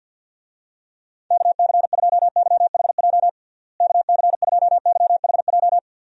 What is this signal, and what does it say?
Morse code, heard as a keyed tone, about 51 wpm, 700 Hz. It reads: KC2YHJ KC2YHJ
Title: USA Call Signs - 6 Characters - Repeated 3x - 50wpm